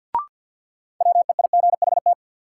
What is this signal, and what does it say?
Morse code, heard as a keyed tone, about 53 wpm, 700 Hz. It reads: WEIGHT
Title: Sentences from Top 800 Words - New Word Drill - 50wpm